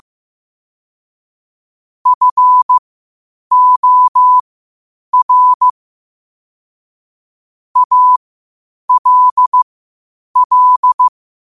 Morse code, heard as a keyed tone, about 15 wpm, spaced as for 8 wpm, 1000 Hz. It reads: FOR ALL
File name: 8wpm